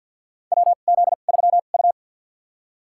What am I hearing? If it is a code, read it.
Morse code, heard as a keyed tone, about 48 wpm, spaced as for 35 wpm, 700 Hz. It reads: WZ3U
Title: USA Call Signs - 4 Characters - 2x Character Spacing - 50wpm